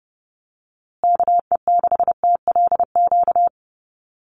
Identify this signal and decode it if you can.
Morse code, heard as a keyed tone, about 30 wpm, 700 Hz. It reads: KE6TLQ
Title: USA Call Signs Repeated 2x 30wpm